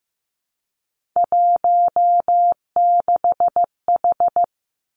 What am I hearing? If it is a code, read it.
Morse code, heard as a keyed tone, about 15 wpm, 700 Hz. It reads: 16H
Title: Three Challenging Letter-Number Combinations 15wpm